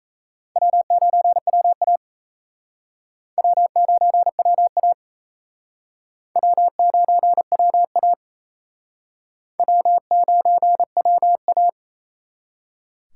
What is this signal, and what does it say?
Morse code, heard as a keyed tone, about 36 wpm, 700 Hz. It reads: W9WA W9WA W9WA W9WA